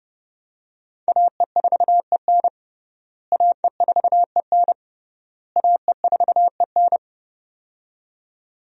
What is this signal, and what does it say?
Morse code, heard as a keyed tone, about 30 wpm, 700 Hz. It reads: AE4EN AE4EN AE4EN
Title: USA Call Signs - 5 Characters - Repeated 3x - 30wpm